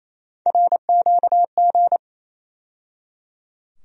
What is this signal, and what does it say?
Morse code, heard as a keyed tone, about 28 wpm, 700 Hz. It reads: RQG
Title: USA Call Sign Postfixes - 3 Letters 28wpm